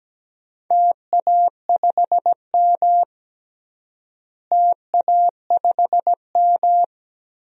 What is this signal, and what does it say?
Morse code, heard as a keyed tone, about 17 wpm, 700 Hz. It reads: TA5M TA5M
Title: DXCC-100 Call Signs - 4 Characters - Repeated 2x - 17wpm